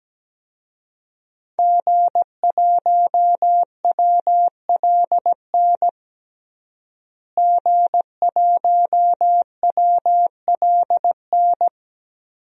Morse code, heard as a keyed tone, about 17 wpm, 700 Hz. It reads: G1WLN G1WLN
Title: DXCC-100 Call Signs - 5 Characters - Repeated 3x - 17wpm